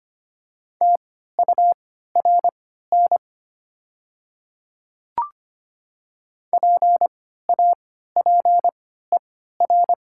Morse code, heard as a keyed tone, about 25 wpm, spaced as for 14 wpm, 700 Hz. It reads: TURN PAPER
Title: 3x Character Spacing - Top 300 Words - 25wpm